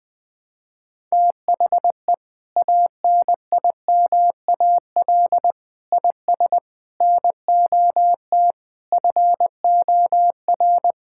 Morse code, heard as a keyed tone, about 20 wpm, 700 Hz. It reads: THE ANIMAL IS NOT FOR
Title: Sentences from Top 1000 Words - Review Part 5 - 20wpm